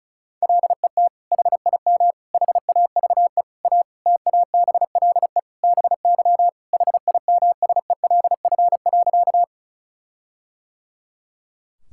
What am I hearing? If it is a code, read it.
Morse code, heard as a keyed tone, about 34 wpm, 700 Hz. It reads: LET HIM HAVE A TABLE BY HIMSELF.
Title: Sentences from Top 1600 Words 35wpm